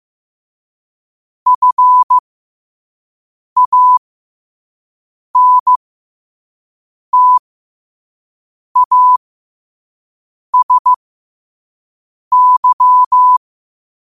Morse code, heard as a keyed tone, about 15 wpm, spaced as for 5 wpm, 1000 Hz. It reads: FANTASY
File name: qotdfiles / 5wpm